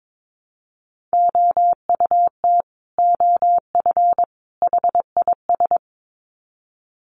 Morse code, heard as a keyed tone, about 22 wpm, 700 Hz. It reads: OUT OF HIS
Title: Sets of 3 Words 22wpm